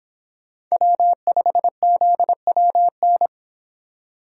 Morse code, heard as a keyed tone, about 25 wpm, 700 Hz. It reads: W5ZWN